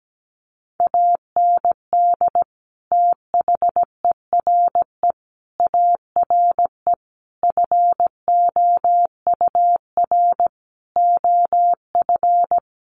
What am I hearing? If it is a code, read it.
Morse code, heard as a keyed tone, about 17 wpm, 700 Hz. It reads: AND THERE ARE FOUR OF
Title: Sentences from Top 300 Words 17wpm